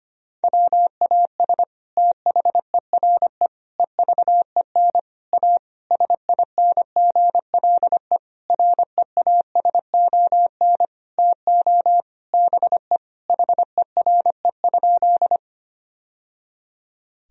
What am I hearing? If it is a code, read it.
Morse code, heard as a keyed tone, about 25 wpm, 700 Hz. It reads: WAS THERE EVEN A SINGLE REASON TO BE HERE?
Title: Sentences from Top 1000 Words - Review Part 4 - 25wpm